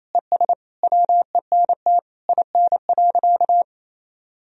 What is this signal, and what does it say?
Morse code, heard as a keyed tone, about 28 wpm, 700 Hz. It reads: ES WENT IN.
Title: Sentences from Top 300 Words and Abbreviations 28wpm